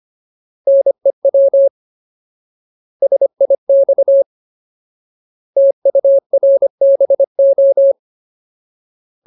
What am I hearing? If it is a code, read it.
Morse code, heard as a keyed tone, about 25 wpm, 550 Hz. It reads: NEW SIX TURBO
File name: Ultimate Word List / 25WPM Ultimate